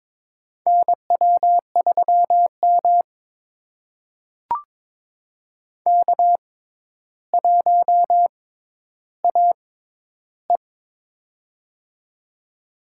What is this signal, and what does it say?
Morse code, heard as a keyed tone, about 22 wpm, spaced as for 11 wpm, 700 Hz. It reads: NW3M K1AE
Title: USA Call Signs - 4 Characters - 6x Character Spacing - 22wpm